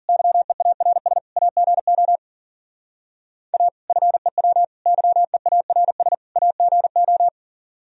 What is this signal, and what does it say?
Morse code, heard as a keyed tone, about 50 wpm, 700 Hz. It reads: YEARS AGO A FEW YEARS AGO
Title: Speed-Racing - Sets of 4 Words 40wpm